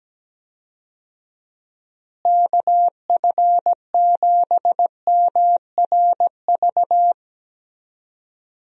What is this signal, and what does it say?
Morse code, heard as a keyed tone, about 17 wpm, 700 Hz. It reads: KF7MRV
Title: USA Call Signs - 6 Characters - Repeated 2x - 17wpm